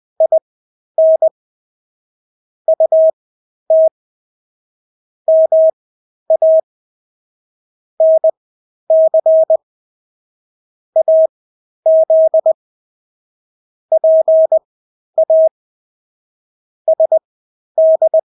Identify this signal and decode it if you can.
Morse code, heard as a keyed tone, about 20 wpm, spaced as for 11 wpm, 640 Hz. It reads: IN UT MA NC AZ PA SD
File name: State-Abbreviations-10